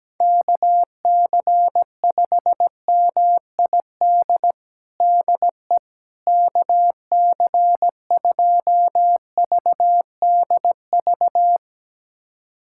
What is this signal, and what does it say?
Morse code, heard as a keyed tone, about 17 wpm, 700 Hz. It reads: KC5MID DE KC2VDV
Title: USA Callsign DE Callsign - 6 Characters - 17wpm